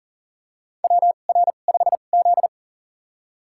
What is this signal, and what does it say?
Morse code, heard as a keyed tone, about 40 wpm, spaced as for 29 wpm, 700 Hz. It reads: WR5Z
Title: USA Call Signs - 4 Characters - 2x Character Spacing - 40wpm